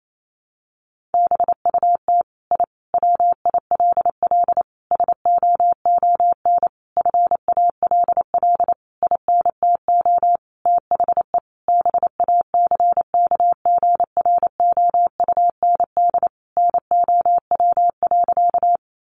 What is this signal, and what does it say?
Morse code, heard as a keyed tone, about 28 wpm, 700 Hz. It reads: BUT I WILL SOON FALL INTO THE BACKGROUND NOW.
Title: Sentences from Top 1300 Words 28wpm